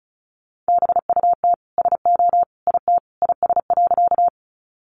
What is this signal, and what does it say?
Morse code, heard as a keyed tone, about 34 wpm, 700 Hz. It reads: BUT SO IT IS.
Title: Sentences from Top 100 Words 35wpm